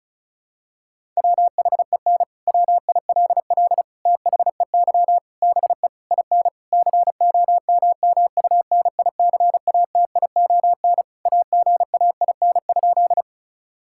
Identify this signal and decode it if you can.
Morse code, heard as a keyed tone, about 36 wpm, 700 Hz. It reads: WHEN WILL THEY BE IN COMMUNICATION AGAIN?
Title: Sentences from Top 1000 Words - Review Part 6 - 35wpm